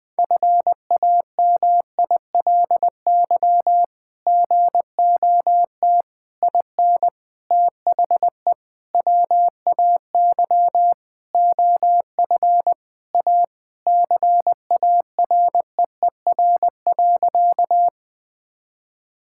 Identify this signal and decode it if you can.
Morse code, heard as a keyed tone, about 20 wpm, 700 Hz. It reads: FAMILY GOT IN THE WAY OF A CAREER.
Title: Sentences from Top 1000 Words - Review Part 6 - 20wpm